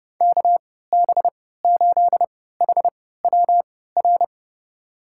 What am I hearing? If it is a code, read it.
Morse code, heard as a keyed tone, about 30 wpm, spaced as for 17 wpm, 700 Hz. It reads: KB8HWR
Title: USA Call Signs - 6 Characters - 3x Character Spacing - 30wpm